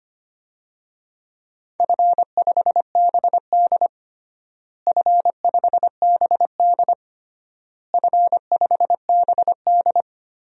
Morse code, heard as a keyed tone, about 25 wpm, 700 Hz. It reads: F5BD F5BD F5BD
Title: DXCC-100 Call Signs - 4 Characters - Repeated 3x - 25wpm